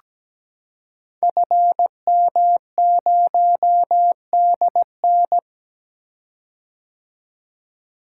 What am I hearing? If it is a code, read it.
Morse code, heard as a keyed tone, about 17 wpm, 700 Hz. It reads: FM0DN
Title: Code Groups - 5 Letter:Numbers - 17wpm